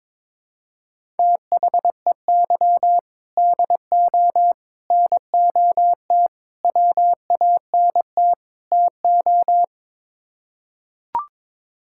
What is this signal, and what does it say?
Morse code, heard as a keyed tone, about 22 wpm, 700 Hz. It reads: THEY DO NOT WANT TO
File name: practice / 22wpm / Build A Phrase - 5-Word Phrases - 22wpm